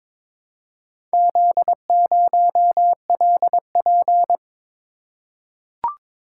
Morse code, heard as a keyed tone, about 22 wpm, 700 Hz. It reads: Z0LP
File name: practice / 22wpm / Four Letter-Number Combinations - 22wpm